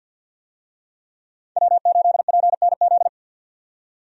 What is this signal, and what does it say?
Morse code, heard as a keyed tone, about 47 wpm, 700 Hz. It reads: W8PNZ
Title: USA Call Signs - 5 Characters - 50wpm